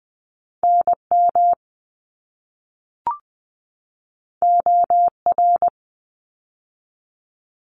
Morse code, heard as a keyed tone, about 20 wpm, 700 Hz. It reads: NM OR
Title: ARRL Sections 20wpm